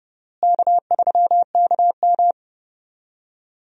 Morse code, heard as a keyed tone, about 30 wpm, 700 Hz. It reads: K3KM
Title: USA Call Signs - 4 Characters - Repeated 3x - 30wpm